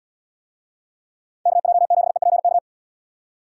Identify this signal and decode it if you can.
Morse code, heard as a keyed tone, about 78 wpm, 700 Hz. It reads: KJ7EPK